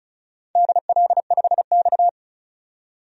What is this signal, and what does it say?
Morse code, heard as a keyed tone, about 35 wpm, 700 Hz. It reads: DL5X
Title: DXCC-100 Call Signs - 4 Characters - 35wpm